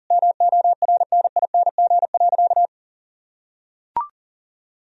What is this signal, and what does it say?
Morse code, heard as a keyed tone, about 40 wpm, 700 Hz. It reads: MORNING.
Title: Sentences from Top 1000 Words - Review Part 2 - 40wpm